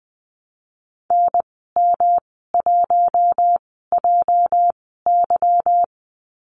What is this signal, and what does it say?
Morse code, heard as a keyed tone, about 20 wpm, spaced as for 15 wpm, 700 Hz. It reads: NM1JY
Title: 207 Lesson 33g - Letter j - Call Signs 20wpm